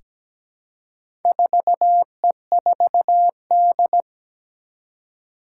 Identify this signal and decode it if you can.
Morse code, heard as a keyed tone, about 17 wpm, 700 Hz. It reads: 4E4D
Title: Four Challenging Letter-Number Combinations 17wpm